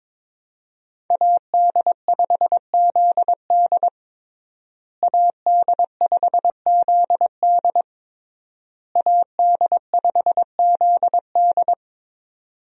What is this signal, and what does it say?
Morse code, heard as a keyed tone, about 22 wpm, 700 Hz. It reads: AD5ZD AD5ZD AD5ZD
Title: USA Call Signs - 5 Characters - Repeated 3x - 22wpm